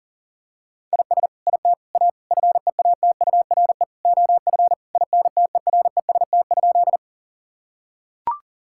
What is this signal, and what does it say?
Morse code, heard as a keyed tone, about 40 wpm, 700 Hz. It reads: IS IT A FEATURE OF INTEREST?
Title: Sentences from Top 1000 Words - Review Part 5 - 40wpm